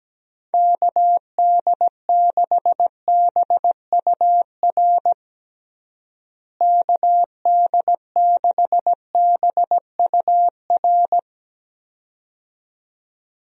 Morse code, heard as a keyed tone, about 17 wpm, 700 Hz. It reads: KD6BUR KD6BUR
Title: USA Call Signs - 6 Characters - Repeated 2x - 17wpm